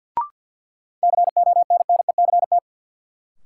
Morse code, heard as a keyed tone, about 50 wpm, 700 Hz. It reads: CONNECT
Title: Sentences from Top 1500 Words - New Word Drill - 50wpm